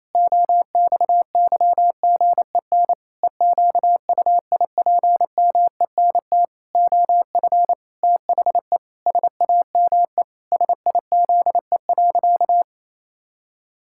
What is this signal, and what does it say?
Morse code, heard as a keyed tone, about 28 wpm, 700 Hz. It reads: OXYGEN EQUIPMENT OF THE SAME SIZE.
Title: Pangram Challenge v2 - 28wpm